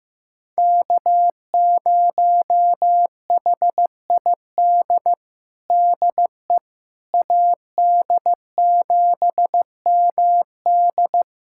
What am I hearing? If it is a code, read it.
Morse code, heard as a keyed tone, about 15 wpm, 700 Hz. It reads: K0HID DE AD7MD
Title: USA Callsign DE Callsign - 5 Characters - 15wpm